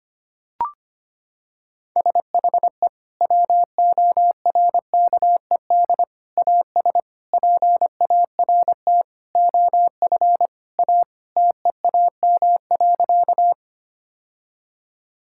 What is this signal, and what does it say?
Morse code, heard as a keyed tone, about 25 wpm, 700 Hz. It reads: SHE WORKED AS PART OF A TEAM.
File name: practice / 25wpm / Sentences with 4-Word Phrases - 25wpm